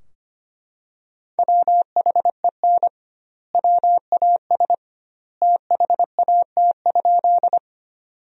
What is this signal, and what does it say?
Morse code, heard as a keyed tone, about 25 wpm, 700 Hz. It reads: WHEN WAS THAT?
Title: 2x Word-Spacing - Sentences from Top 100 Words - 25wpm